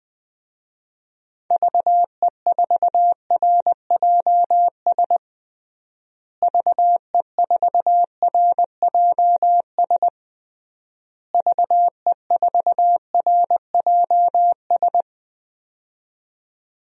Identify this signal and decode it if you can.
Morse code, heard as a keyed tone, about 21 wpm, 700 Hz. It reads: VE4RJS VE4RJS VE4RJS
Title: DXCC-100 Call Signs - 6 Characters - Repeated 3x - 20wpm